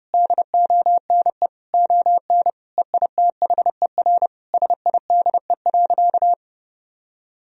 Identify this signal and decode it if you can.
Morse code, heard as a keyed tone, about 32 wpm, 700 Hz. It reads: DONE ON EITHER SIDE.
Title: Sentences from Top 1000 Words - Review Part 4 - 30wpm